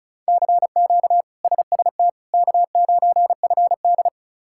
Morse code, heard as a keyed tone, about 36 wpm, 700 Hz. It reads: CQ SST K9FD
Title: K1USN SST QSO Practice - 35wpm